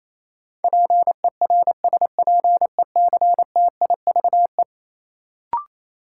Morse code, heard as a keyed tone, about 28 wpm, 700 Hz. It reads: PERSPECTIVE
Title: Sentences from Top 1200 Words - New Word Drill Alt - 28wpm